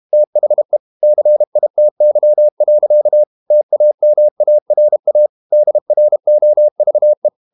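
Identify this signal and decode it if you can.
Morse code, heard as a keyed tone, about 32 wpm, 600 Hz. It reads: THE CITY. TAMARA DROVE
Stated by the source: Morse code practice at 32 wpm from CW med Gnister, reading Ken Follett's novel Never in English.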